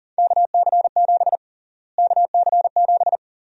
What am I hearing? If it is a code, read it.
Morse code, heard as a keyed tone, about 40 wpm, 700 Hz. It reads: KC7 KC7
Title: USA Call Sign Prefixes - Repeated 2x - 40wpm